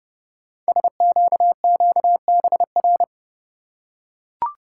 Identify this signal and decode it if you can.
Morse code, heard as a keyed tone, about 28 wpm, 690 Hz. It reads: SQQBR
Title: Code Groups - 5 Letters - 30wpm